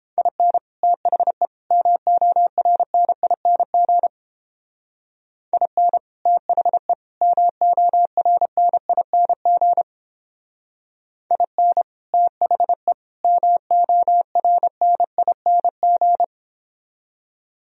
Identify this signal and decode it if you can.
Morse code, heard as a keyed tone, about 31 wpm, 700 Hz. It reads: IN THE MORNING IN THE MORNING IN THE MORNING